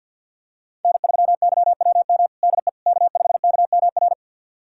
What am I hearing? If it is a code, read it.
Morse code, heard as a keyed tone, about 46 wpm, 690 Hz. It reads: N3YWM DE K5KMR
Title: USA Callsign DE Callsign - 5 Characters - 50wpm